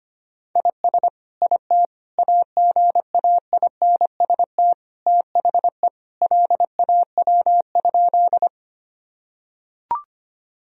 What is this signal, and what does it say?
Morse code, heard as a keyed tone, about 25 wpm, 700 Hz. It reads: IS IT AGAINST THE LAW?
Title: Sentences from Top 1000 Words - Review Part 3 - 25wpm